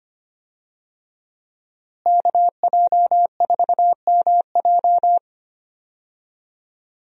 Morse code, heard as a keyed tone, about 25 wpm, 700 Hz. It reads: KJ4MJ
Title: USA Call Signs - 5 Characters - Repeated 2x - 25wpm